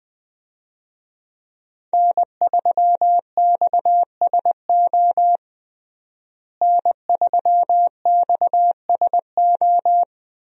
Morse code, heard as a keyed tone, about 20 wpm, 700 Hz. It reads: N3XSO N3XSO
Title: USA Call Signs - 5 Characters - Repeated 3x - 20wpm